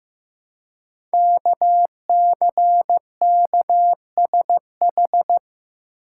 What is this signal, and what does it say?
Morse code, heard as a keyed tone, about 15 wpm, 700 Hz. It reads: KCKSH
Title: Code Groups - 5 Letters - 15wpm